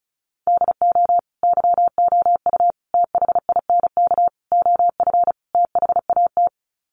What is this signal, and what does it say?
Morse code, heard as a keyed tone, about 34 wpm, 700 Hz. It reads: DO YOU THINK OF THAT
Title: Sets of 5 Words Encore2 35wpm